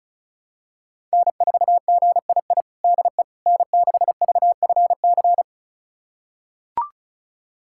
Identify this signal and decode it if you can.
Morse code, heard as a keyed tone, about 36 wpm, 700 Hz. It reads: N4GII DE N6VFC
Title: USA Callsign DE Callsign - 5 Characters - 35wpm